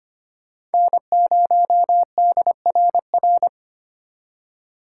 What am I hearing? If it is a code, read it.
Morse code, heard as a keyed tone, about 32 wpm, 700 Hz. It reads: N0DRR